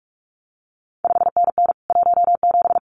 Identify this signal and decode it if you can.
Morse code, heard as a keyed tone, about 46 wpm, 700 Hz. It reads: 5NN 17
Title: CQ WW DX Contest Exchange Encore - 45wpm